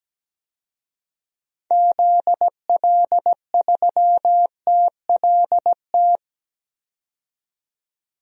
Morse code, heard as a keyed tone, about 17 wpm, 700 Hz. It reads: ZL3TLT
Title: DXCC-100 Call Signs - 6 Characters - 17wpm